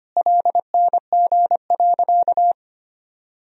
Morse code, heard as a keyed tone, about 25 wpm, 700 Hz. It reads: LNG.
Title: Sentences from Top 100 Words and Abbreviations 25wpm